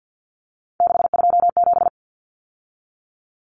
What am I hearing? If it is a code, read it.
Morse code, heard as a keyed tone, about 50 wpm, 700 Hz. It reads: 627
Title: Numbers - 3-Digit 50wpm